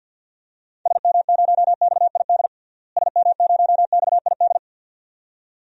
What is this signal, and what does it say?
Morse code, heard as a keyed tone, about 50 wpm, 700 Hz. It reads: SM0XID SM0XID
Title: DXCC-100 Call Signs - 6 Characters - Repeated 2x - 50wpm